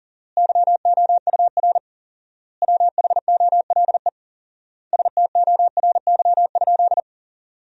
Morse code, heard as a keyed tone, about 40 wpm, 700 Hz. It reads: YOUR WHOLE STORY?
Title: 4x Word-Spacing - Sentences from Top 400 Words - 40wpm